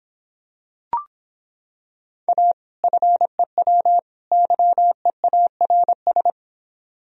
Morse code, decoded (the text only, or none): A FEW YEARS